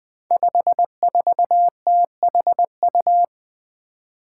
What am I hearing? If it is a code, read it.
Morse code, heard as a keyed tone, about 20 wpm, 700 Hz. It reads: 54THU
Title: Five Challenging Letter-Number Combinations 20wpm